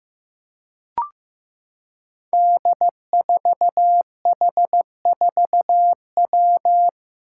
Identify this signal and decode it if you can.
Morse code, heard as a keyed tone, about 15 wpm, 700 Hz. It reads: D4H4W
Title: Five Challenging Letter-Number Combinations 15wpm